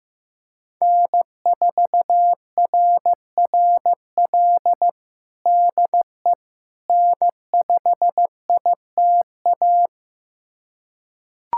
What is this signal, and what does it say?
Morse code, heard as a keyed tone, about 15 wpm, 700 Hz. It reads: N4RRL DE N5ITA
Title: USA Callsign DE Callsign - 5 Characters - 15wpm